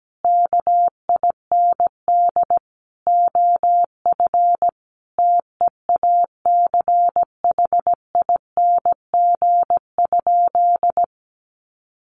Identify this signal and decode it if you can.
Morse code, heard as a keyed tone, about 17 wpm, 700 Hz. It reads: KIND OF TEACHING?
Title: Sentences from Top 1100 Words 17wpm